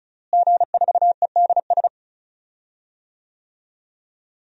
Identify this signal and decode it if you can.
Morse code, heard as a keyed tone, about 36 wpm, 700 Hz. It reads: G4EDS
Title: DXCC-100 Call Signs - 5 Characters - Repeated 2x - 35wpm